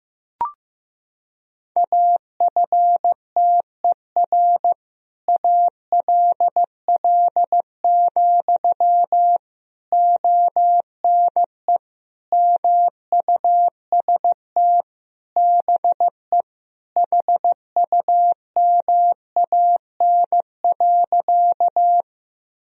Morse code, heard as a keyed tone, about 15 wpm, 700 Hz. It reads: AFTER ALL, ONE MUST BE HUMAN.
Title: Sentences from Top 1000 Words - Review Part 3 - 15wpm